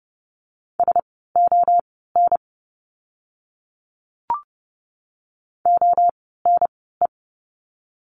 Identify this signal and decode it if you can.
Morse code, heard as a keyed tone, about 30 wpm, spaced as for 17 wpm, 700 Hz. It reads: SON ONE
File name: Lesson 17f - Letter u - Words 30wpm